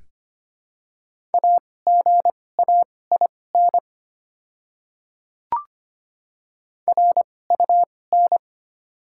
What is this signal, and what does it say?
Morse code, heard as a keyed tone, about 25 wpm, spaced as for 18 wpm, 700 Hz. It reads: AGAIN RUN
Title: 2x Character Spacing - Top 300 Words - 25wpm